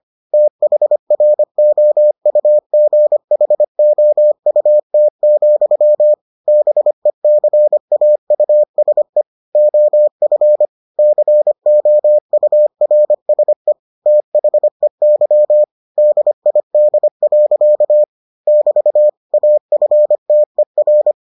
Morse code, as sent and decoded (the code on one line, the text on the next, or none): - .... .-. --- ..- --. .... --- ..- - --..-- / -... -.-. .- ..- ... / --- ..-. / -.-. --- ..- .-. ... / - .... -.-- / -.. .. -.. .-.-.- / -...- / .- ..-. - . .-.
THROUGHOUT, BECAUSE OF COURSE THEY DID. <BT> AFTER